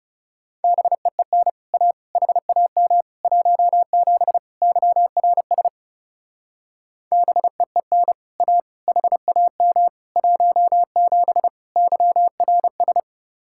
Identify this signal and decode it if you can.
Morse code, heard as a keyed tone, about 36 wpm, 700 Hz. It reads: BEEN A HAM 17 YRS BEEN A HAM 17 YRS
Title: Speed-Racing Common QSO Phrases - 30wpm